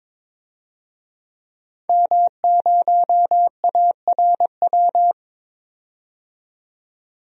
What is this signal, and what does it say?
Morse code, heard as a keyed tone, about 22 wpm, 700 Hz. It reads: M0ARW
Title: DXCC-100 Call Signs - 5 Characters - 22wpm